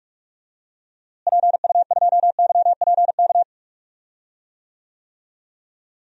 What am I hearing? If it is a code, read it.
Morse code, heard as a keyed tone, about 44 wpm, 700 Hz. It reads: PU2YPK
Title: DXCC-100 Call Signs - 6 Characters - Repeated 2x - 45wpm